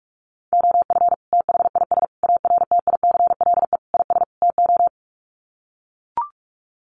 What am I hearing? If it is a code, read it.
Morse code, heard as a keyed tone, about 44 wpm, 700 Hz. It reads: OF THIS ARTICLE IS TO
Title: Sets of 5 Words Encore2 45wpm